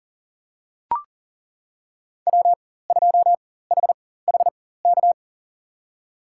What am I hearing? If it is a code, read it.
Morse code, heard as a keyed tone, about 38 wpm, spaced as for 18 wpm, 700 Hz. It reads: W2HHK
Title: USA Call Signs - 5 Characters - 4x Character Spacing - 40wpm